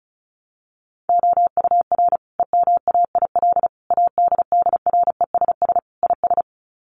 Morse code, heard as a keyed tone, about 34 wpm, 700 Hz. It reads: OUR EMAIL ADDRESS IS
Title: Sets of 5 Words Encore2 35wpm